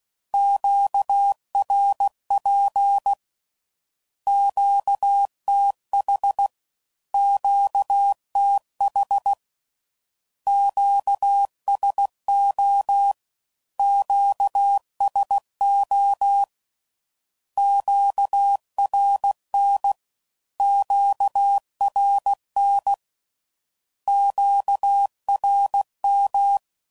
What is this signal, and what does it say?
Morse code, heard as a keyed tone, about 16 wpm, 800 Hz. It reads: QRP QTH QTH QSO QSO QRN QRN QRM
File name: CW Code Groups @15WPM/Q Calls/Q Calls Twice